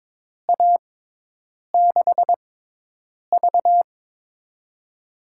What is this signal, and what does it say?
Morse code, heard as a keyed tone, about 23 wpm, spaced as for 8 wpm, 700 Hz. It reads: A6V
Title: USA Call Signs - 4 Characters - 6x Character Spacing - 22wpm